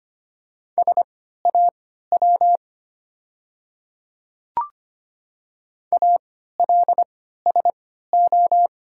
Morse code, heard as a keyed tone, about 25 wpm, spaced as for 14 wpm, 700 Hz. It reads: SAW ALSO
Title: 3x Character Spacing - Top 300 Words - 25wpm